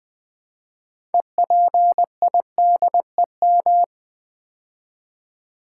Morse code, heard as a keyed tone, about 20 wpm, 700 Hz. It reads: EPIDEM
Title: Build A Word - 2 Letters At A Time - 8-Letter Words - 20wpm